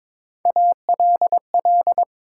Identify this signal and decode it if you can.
Morse code, heard as a keyed tone, about 22 wpm, 700 Hz. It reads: ALL
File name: practice / 22wpm / Build A Phrase - 5-Word Phrases - 22wpm